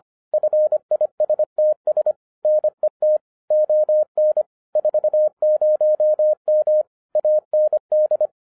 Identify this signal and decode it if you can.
Morse code, heard as a keyed tone, about 26 wpm, 610 Hz. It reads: FISTS NET ON 40M AND
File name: 2011-10-mp3 / n04-JUST MOVED 25wpm